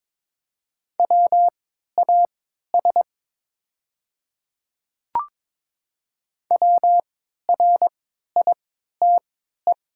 Morse code, heard as a keyed tone, about 22 wpm, spaced as for 13 wpm, 690 Hz. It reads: WAS WRITE
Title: 3x Character Spacing - Top 100 Words - 22wpm